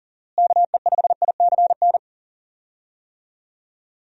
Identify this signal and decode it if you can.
Morse code, heard as a keyed tone, about 40 wpm, 700 Hz. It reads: KE5ICN
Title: USA Call Signs - 6 Characters - Repeated 2x - 40wpm